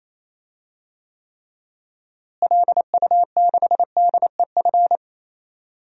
Morse code, heard as a keyed tone, about 28 wpm, 700 Hz. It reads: LU6DEF
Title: DXCC-100 Call Signs - 6 Characters - Repeated 2x - 28wpm